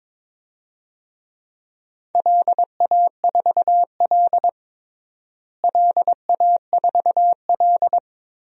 Morse code, heard as a keyed tone, about 22 wpm, 700 Hz. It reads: LA4L LA4L
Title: DXCC-100 Call Signs - 4 Characters - Repeated 3x - 22wpm